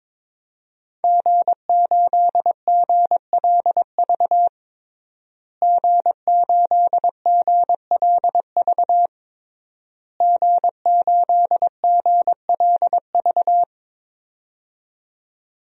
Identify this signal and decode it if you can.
Morse code, heard as a keyed tone, about 22 wpm, 700 Hz. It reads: G8GLV G8GLV G8GLV
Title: DXCC-100 Call Signs - 5 Characters - Repeated 3x - 22wpm